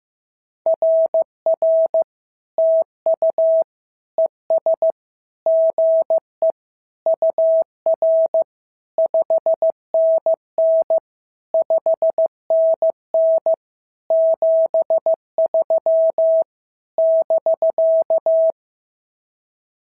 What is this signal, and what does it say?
Morse code, heard as a keyed tone, about 14 wpm, 650 Hz. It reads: RR TU ES GE UR 5NN 5NN 73 <BK>
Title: SOTA QSO Practice 15wpm